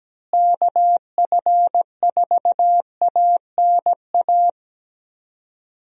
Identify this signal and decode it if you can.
Morse code, heard as a keyed tone, about 17 wpm, 700 Hz. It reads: KF4ANA
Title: USA Call Signs - 6 Characters - Repeated 3x - 17wpm